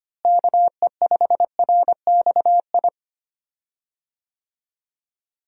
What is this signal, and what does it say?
Morse code, heard as a keyed tone, about 25 wpm, 700 Hz. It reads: KE5RXI
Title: USA Call Signs - 6 Characters - Repeated 3x - 25wpm